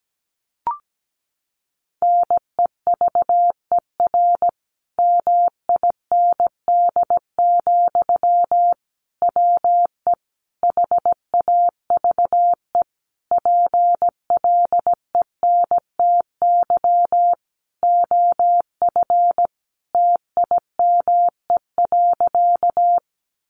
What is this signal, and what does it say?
Morse code, heard as a keyed tone, about 17 wpm, 700 Hz. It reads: NEVER MIND, WE HAVE PLENTY OF TIME.
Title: Sentences from Top 1300 Words 17wpm